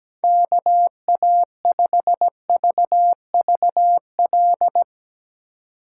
Morse code, heard as a keyed tone, about 18 wpm, 700 Hz. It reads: KA5VVL
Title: Speed Racing - USA Call Signs 17wpm